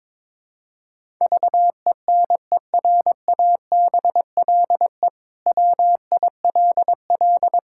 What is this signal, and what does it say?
Morse code, heard as a keyed tone, about 22 wpm, 700 Hz. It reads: VENERABLE WILL
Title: Pangram Challenge v2 - 22wpm